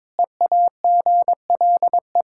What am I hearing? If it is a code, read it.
Morse code, heard as a keyed tone, about 21 wpm, 700 Hz. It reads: EAGLE
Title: Practice QSO - Exchange 2 - Info - 22wpm